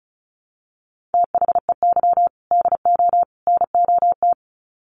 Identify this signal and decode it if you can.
Morse code, heard as a keyed tone, about 34 wpm, 700 Hz. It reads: THEY DO NOT
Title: Sets of 3 Words 35wpm